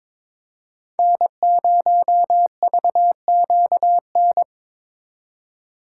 Morse code, heard as a keyed tone, about 26 wpm, 700 Hz. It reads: N0VQN
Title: Speed Racing - USA 1x3 Call Signs 22wpm